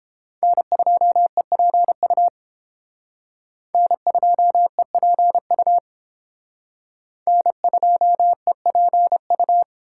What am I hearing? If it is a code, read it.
Morse code, heard as a keyed tone, about 29 wpm, 700 Hz. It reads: N2EPU N2EPU N2EPU